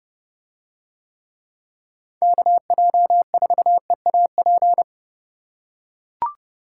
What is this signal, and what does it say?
Morse code, heard as a keyed tone, about 28 wpm, 700 Hz. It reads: KJ4EAP